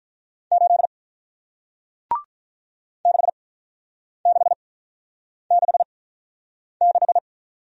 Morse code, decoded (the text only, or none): 8 6666